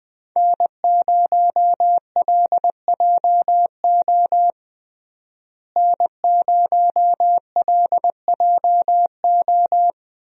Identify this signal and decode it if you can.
Morse code, heard as a keyed tone, about 21 wpm, 700 Hz. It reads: N0LJO N0LJO
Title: USA Call Signs - 5 Characters - Repeated 3x - 20wpm